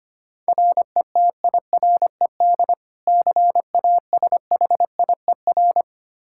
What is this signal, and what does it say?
Morse code, heard as a keyed tone, about 25 wpm, 700 Hz. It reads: RETIRED CASHIER
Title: K1USN SST QSO Practice 25wpm